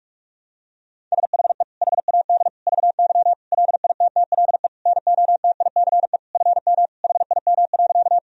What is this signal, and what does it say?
Morse code, heard as a keyed tone, about 46 wpm, 700 Hz. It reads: SHE HAD VY LITTLE NOTICE FM HIM.
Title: Sentences from Top 400 Words and Abbreviations 45wpm